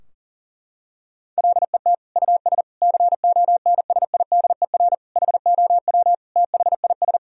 Morse code, heard as a keyed tone, about 40 wpm, 700 Hz. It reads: LET US CONSIDER HOW THIS
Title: Sentences from Top 1000 Words - Review Part 4 - 40wpm